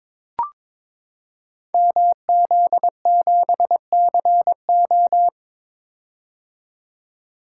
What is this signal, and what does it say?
Morse code, heard as a keyed tone, about 21 wpm, 700 Hz. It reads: MZ7CO
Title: Code Groups - 5 Letter:Numbers - 22wpm